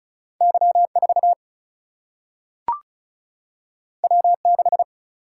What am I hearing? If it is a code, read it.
Morse code, heard as a keyed tone, about 36 wpm, 700 Hz. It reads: Y 4 W 6
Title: Single-Letter Single-Number 35wpm